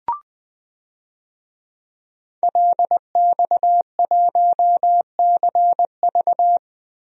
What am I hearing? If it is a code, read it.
Morse code, heard as a keyed tone, about 20 wpm, 700 Hz. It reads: LX1CV
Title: DXCC-100 Call Signs - 5 Characters - 20wpm